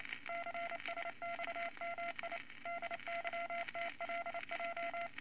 Morse code, heard as a keyed tone, about 28 wpm, 680 Hz, with noise. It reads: CSXMI DYTLJ
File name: Morse Code Records / 22 A LCM 172 Twenty Words Per Minute